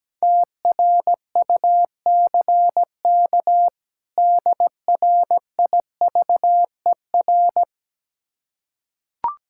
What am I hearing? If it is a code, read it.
Morse code, heard as a keyed tone, about 17 wpm, 700 Hz. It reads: TRUCK DRIVER
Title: Practice QSO - Exchange 3 - More Info - 17wpm